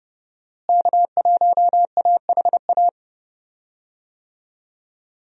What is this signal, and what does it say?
Morse code, heard as a keyed tone, about 30 wpm, 700 Hz. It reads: K1AHA
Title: USA Call Signs - 5 Characters - Repeated 3x - 30wpm